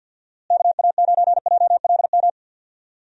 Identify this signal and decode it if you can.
Morse code, heard as a keyed tone, about 47 wpm, 700 Hz. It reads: KA9JLM
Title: USA Call Signs - 6 Characters - Repeated 2x - 50wpm